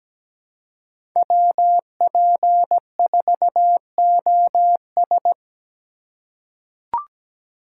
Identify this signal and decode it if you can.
Morse code, heard as a keyed tone, about 17 wpm, 700 Hz. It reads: WP4OS
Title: USA Call Signs - 5 Characters - 4x Character Spacing - 17wpm